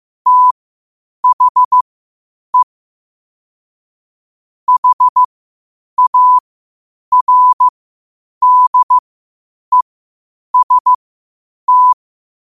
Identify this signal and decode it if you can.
Morse code, heard as a keyed tone, about 15 wpm, spaced as for 8 wpm, 1000 Hz. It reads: THE HARDEST
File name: qotdfiles / 8wpm